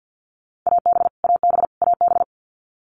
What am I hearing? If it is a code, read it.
Morse code, heard as a keyed tone, about 50 wpm, 700 Hz. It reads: AB AB AB
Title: Canadian Provinces and Territories - Repeated 3x - 50wpm